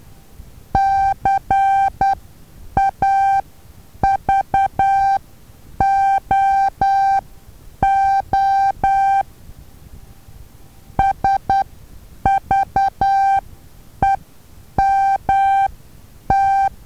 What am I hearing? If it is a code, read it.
Morse code, heard as a keyed tone, about 10 wpm, spaced as for 8 wpm, 780 Hz. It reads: CAVOO SVEMT
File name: Taviro HA6VA A